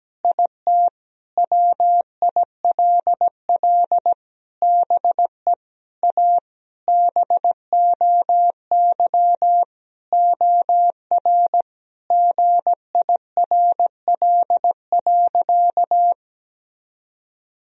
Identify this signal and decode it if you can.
Morse code, heard as a keyed tone, about 17 wpm, 700 Hz. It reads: IT WILL BE A BOY OR GIRL.
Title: Sentences from Top 1000 Words - Review Part 3 - 17wpm